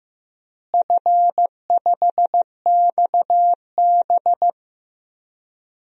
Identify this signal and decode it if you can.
Morse code, heard as a keyed tone, about 16 wpm, 700 Hz. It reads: F5XB
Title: DXCC-100 Call Signs - 4 Characters - Repeated 3x - 15wpm